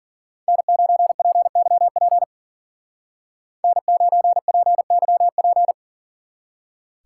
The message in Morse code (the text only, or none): N9PYP N9PYP